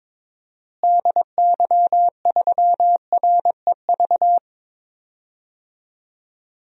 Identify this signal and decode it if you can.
Morse code, heard as a keyed tone, about 22 wpm, 700 Hz. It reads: DY3REV
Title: DXCC-100 Call Signs - 6 Characters - Repeated 3x - 22wpm